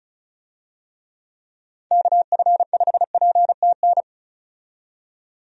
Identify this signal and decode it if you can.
Morse code, heard as a keyed tone, about 36 wpm, 700 Hz. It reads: KF5PTN
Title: USA Call Signs - 6 Characters - 35wpm